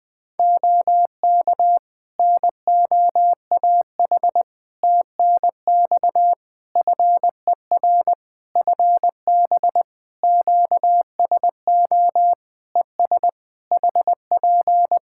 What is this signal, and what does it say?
Morse code, heard as a keyed tone, about 20 wpm, 700 Hz. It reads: OK NOAH TNX FER FB QSO ES HP
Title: Practice QSO - Exchange 4 - Ending - 20wpm